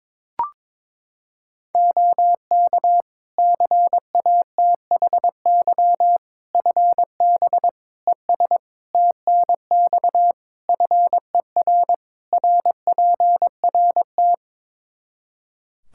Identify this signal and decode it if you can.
Morse code, heard as a keyed tone, about 21 wpm, 700 Hz. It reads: OK CATHY FB ES TNX FER RPRT
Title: Practice QSO - Exchange 2 - Info - 22wpm